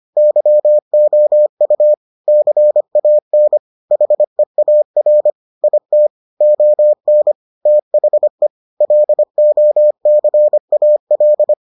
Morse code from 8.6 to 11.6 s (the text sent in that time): LOCAL